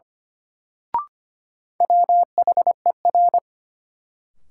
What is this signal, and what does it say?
Morse code, heard as a keyed tone, about 25 wpm, 700 Hz. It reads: WHER
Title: Sentences from Top 1500 Words - New Word Drill - 25wpm